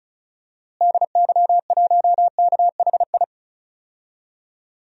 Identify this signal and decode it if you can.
Morse code, heard as a keyed tone, about 36 wpm, 700 Hz. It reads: DY1KHI
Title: DXCC-100 Call Signs - 6 Characters - Repeated 2x - 35wpm